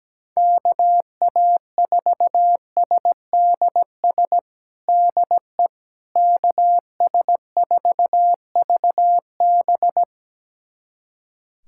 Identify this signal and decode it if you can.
Morse code, heard as a keyed tone, about 17 wpm, 700 Hz. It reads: KA4SDS DE KS4VB
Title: USA Callsign DE Callsign 17wpm